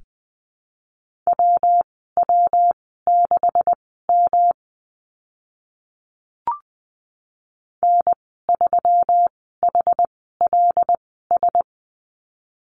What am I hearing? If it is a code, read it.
Morse code, heard as a keyed tone, about 20 wpm, spaced as for 14 wpm, 700 Hz. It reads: WW6M N3HLS
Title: 128 Lesson 21g - Number 6 - Call Signs 20wpm